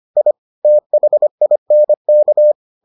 Morse code, heard as a keyed tone, about 25 wpm, 600 Hz. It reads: I THINK